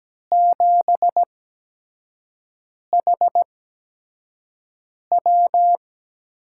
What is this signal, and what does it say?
Morse code, heard as a keyed tone, about 17 wpm, spaced as for 5 wpm, 700 Hz. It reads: 7HW
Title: USA Call Signs - 5 Characters - 8x Character Spacing - 17wpm